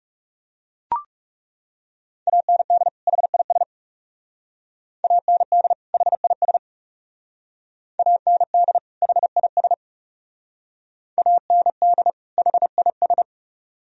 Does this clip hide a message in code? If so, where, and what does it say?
Yes: Morse code from 2.3 to 13.2 s, heard as a keyed tone, about 40 wpm, 700 Hz. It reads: AND HIS AND HIS AND HIS AND HIS